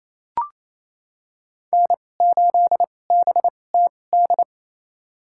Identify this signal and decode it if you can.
Morse code, heard as a keyed tone, about 28 wpm, spaced as for 20 wpm, 700 Hz. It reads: N8BTD
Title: USA Call Signs - 5 Characters - 2x Character Spacing - 28wpm